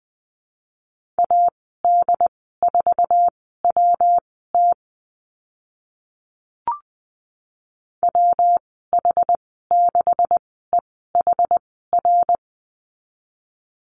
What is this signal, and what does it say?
Morse code, heard as a keyed tone, about 20 wpm, spaced as for 14 wpm, 700 Hz. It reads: AD4WT WH6EHR
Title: 128 Lesson 21g - Number 6 - Call Signs 20wpm